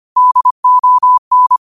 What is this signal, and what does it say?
Morse code, heard as a keyed tone, about 25 wpm, 1000 Hz. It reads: DON